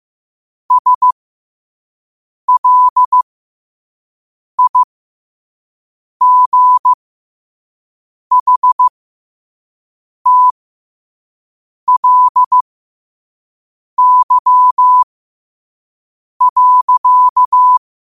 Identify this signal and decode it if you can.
Morse code, heard as a keyed tone, about 15 wpm, spaced as for 5 wpm, 1000 Hz. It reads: SLIGHTLY.
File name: qotdfiles / 5wpm